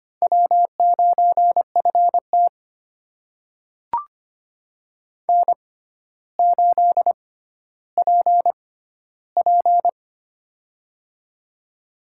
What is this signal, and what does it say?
Morse code, heard as a keyed tone, about 25 wpm, spaced as for 13 wpm, 700 Hz. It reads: W9FT N8PP
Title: USA Call Signs - 4 Characters - 6x Character Spacing - 25wpm